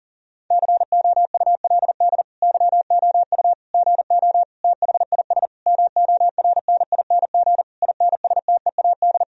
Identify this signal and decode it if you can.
Morse code, heard as a keyed tone, about 40 wpm, 700 Hz. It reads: COULD YOU GO THIS MORNING INSTEAD
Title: Sentences from Top 1000 Words - Review Part 4 - 40wpm